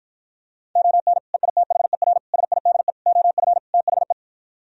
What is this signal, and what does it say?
Morse code, heard as a keyed tone, about 52 wpm, 700 Hz. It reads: ON EITHER SIDE OF THE